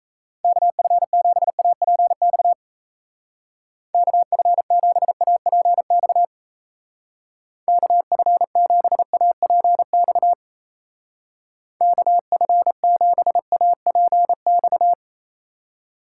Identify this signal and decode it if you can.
Morse code, heard as a keyed tone, about 35 wpm, 700 Hz. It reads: KF7APX KF7APX KF7APX KF7APX